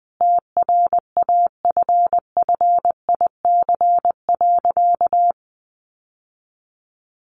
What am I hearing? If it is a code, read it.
Morse code, heard as a keyed tone, about 20 wpm, 700 Hz. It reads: TRAFFIC.
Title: Sentences from Top 1300 Words 20wpm